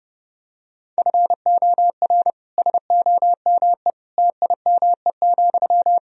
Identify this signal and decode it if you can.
Morse code, heard as a keyed tone, about 30 wpm, 700 Hz. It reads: FOR SOME TIME,
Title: Sentences from Top 1000 Words - Review Part 4 - 30wpm